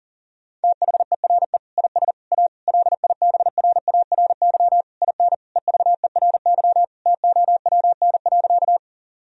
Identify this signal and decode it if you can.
Morse code, heard as a keyed tone, about 37 wpm, 700 Hz. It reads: THERE IS A LIBRARY IN EVERY TOWN.
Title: Sentences from Top 1000 Words - Review Part 4 - 40wpm